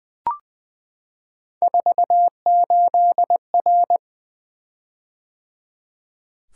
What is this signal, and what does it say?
Morse code, heard as a keyed tone, about 20 wpm, 700 Hz. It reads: 48R
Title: Three Letter-Number Combinations - 20wpm